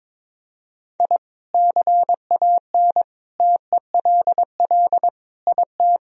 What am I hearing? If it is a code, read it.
Morse code, heard as a keyed tone, about 22 wpm, 700 Hz. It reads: I CAN TELL IT
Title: Sentences from Top 1000 Words - Review Part 1 - 22wpm